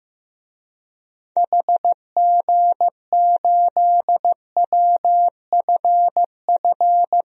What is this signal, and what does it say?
Morse code, heard as a keyed tone, about 15 wpm, 700 Hz. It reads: HG8WFF
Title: DXCC-100 Call Signs - 6 Characters - 15wpm